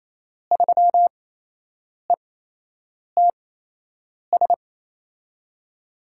Morse code, heard as a keyed tone, about 28 wpm, spaced as for 8 wpm, 700 Hz. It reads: 3ETS
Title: USA Call Signs - 5 Characters - 8x Character Spacing - 28wpm